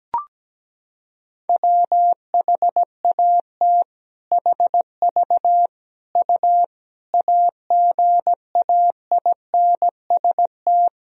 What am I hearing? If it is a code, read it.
Morse code, heard as a keyed tone, about 17 wpm, 700 Hz. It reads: WHAT HV U AGAINST
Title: Sentences from Top 400 Words and Abbreviations 17wpm